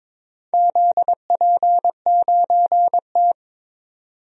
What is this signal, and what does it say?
Morse code, heard as a keyed tone, about 22 wpm, 690 Hz. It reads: ZP9T
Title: DXCC-100 Call Signs - 4 Characters - 22wpm